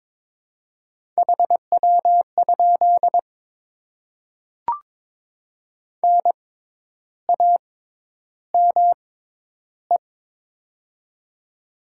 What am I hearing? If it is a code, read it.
Morse code, heard as a keyed tone, about 22 wpm, spaced as for 10 wpm, 700 Hz. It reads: HW? NAME
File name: Top 20 QSO Elements - 6x Character Spacing - 22wpm